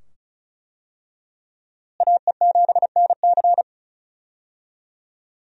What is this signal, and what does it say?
Morse code, heard as a keyed tone, about 34 wpm, 700 Hz. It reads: AE7NC
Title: USA Call Signs - 5 Characters - Repeated 2x - 35wpm